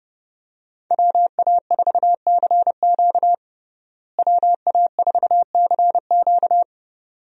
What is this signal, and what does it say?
Morse code, heard as a keyed tone, about 30 wpm, 700 Hz. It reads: WA4CQ WA4CQ
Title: USA Call Signs - 5 Characters - Repeated 2x - 30wpm